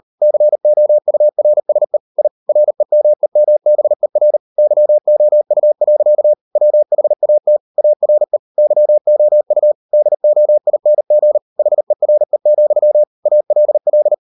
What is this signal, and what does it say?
Morse code, heard as a keyed tone, about 38 wpm, 600 Hz. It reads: COURSE I REMEMBER YOU. WHAT ARE YOU DOING HERE, ALL